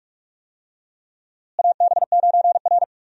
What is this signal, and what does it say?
Morse code, heard as a keyed tone, about 44 wpm, 700 Hz. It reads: AB9R
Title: USA Call Signs - 4 Characters - Repeated 3x - 45wpm